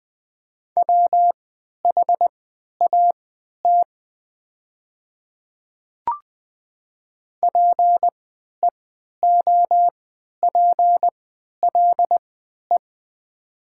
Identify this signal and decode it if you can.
Morse code, heard as a keyed tone, about 20 wpm, spaced as for 11 wpm, 700 Hz. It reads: WHAT PEOPLE
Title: 3x Character Spacing - Top 100 Words - 20wpm